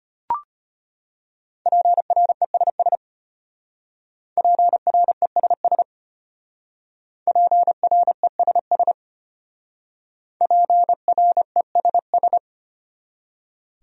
Morse code, heard as a keyed tone, about 34 wpm, 700 Hz. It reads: PRESS PRESS PRESS PRESS